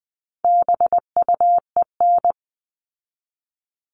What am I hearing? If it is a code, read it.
Morse code, heard as a keyed tone, about 20 wpm, 700 Hz. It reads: BUEN
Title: All Cut Numbers - 4-Digit 20wpm